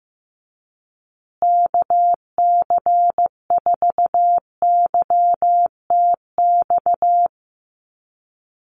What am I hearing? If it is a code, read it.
Morse code, heard as a keyed tone, about 15 wpm, 700 Hz. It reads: KC4YTX
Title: USA Call Signs Repeated 3x 15wpm